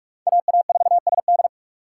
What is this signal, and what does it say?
Morse code, heard as a keyed tone, about 46 wpm, 700 Hz. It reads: AA4SD
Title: USA Call Signs - 5 Characters - 2x Character Spacing - 45wpm